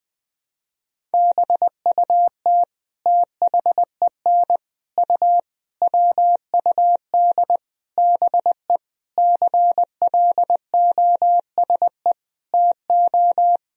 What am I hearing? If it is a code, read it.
Morse code, heard as a keyed tone, about 21 wpm, 700 Hz. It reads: BUT THEN U WUD BE CLOSE TO
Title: Sentences from Top 300 Words and Abbreviations 20wpm